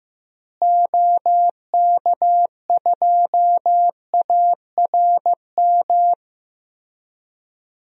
Morse code, heard as a keyed tone, about 15 wpm, 700 Hz. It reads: OK2ARM
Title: DXCC-100 Call Signs - 6 Characters - Repeated 3x - 15wpm